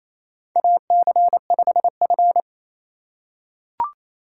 Morse code, heard as a keyed tone, about 28 wpm, 700 Hz. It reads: AC5F
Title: USA Call Signs - 4 Characters - 3x Character Spacing - 28wpm